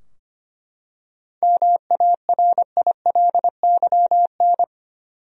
Morse code, heard as a keyed tone, about 25 wpm, 700 Hz. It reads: MARILYN
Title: USA Common Names - 25wpm